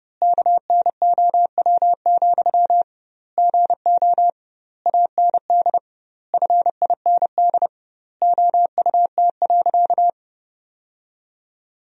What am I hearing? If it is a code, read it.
Morse code, heard as a keyed tone, about 30 wpm, 700 Hz. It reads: KNOW, GO AND FIND OUT.
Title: 2x Word-Spacing - Sentences from Top 200 Words - 30wpm